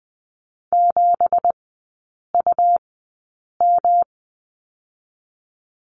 Morse code, heard as a keyed tone, about 20 wpm, spaced as for 8 wpm, 700 Hz. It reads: 7UM
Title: 191 Lesson 31e - Letter v - Three Characters 20wpm